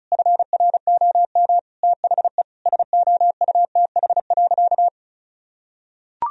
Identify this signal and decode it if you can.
Morse code, heard as a keyed tone, about 36 wpm, 700 Hz. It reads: FROM THE SOUTH.
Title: Sentences from Top 1000 Words - Review Part 2 - 35wpm